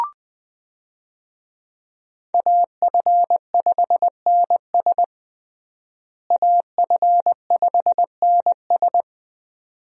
Morse code, heard as a keyed tone, about 21 wpm, 700 Hz. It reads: AF5NS AF5NS
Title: USA Call Signs - 5 Characters - Repeated 2x - 20wpm